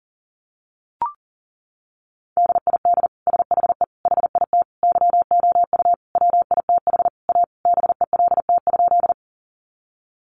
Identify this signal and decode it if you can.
Morse code, heard as a keyed tone, about 40 wpm, 700 Hz. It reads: DID SHE HIT YOU WITH A BELT?
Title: Sentences from Top 1400 Words 40wpm